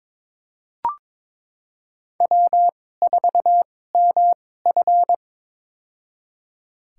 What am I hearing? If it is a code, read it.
Morse code, heard as a keyed tone, about 22 wpm, spaced as for 16 wpm, 700 Hz. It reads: W4MF
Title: USA Call Signs - 4 Characters - 2x Character Spacing - 22wpm